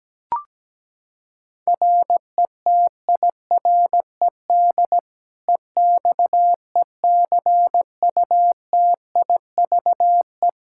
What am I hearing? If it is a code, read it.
Morse code, heard as a keyed tone, about 17 wpm, 700 Hz. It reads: RETIRED EXECUTIVE
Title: Practice QSO - Exchange 3 - More Info - 17wpm